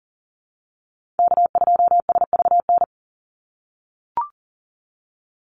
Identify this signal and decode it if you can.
Morse code, heard as a keyed tone, about 40 wpm, 700 Hz. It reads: K2SVN
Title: USA Call Signs 40wpm